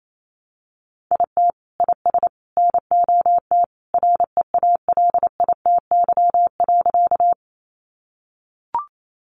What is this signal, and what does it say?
Morse code, heard as a keyed tone, about 28 wpm, 700 Hz. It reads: IT IS NOT REALITY.
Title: Sentences from Top 1200 Words 28wpm